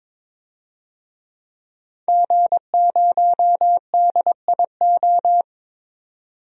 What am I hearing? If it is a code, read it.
Morse code, heard as a keyed tone, about 21 wpm, 700 Hz. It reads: G0DIO
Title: DXCC-100 Call Signs - 5 Characters - 22wpm